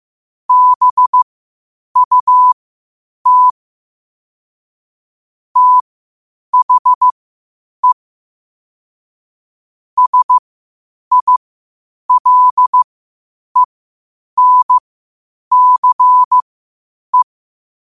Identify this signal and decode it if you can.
Morse code, heard as a keyed tone, about 15 wpm, spaced as for 8 wpm, 1000 Hz. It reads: BUT THE SILENCE
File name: qotdfiles / 8wpm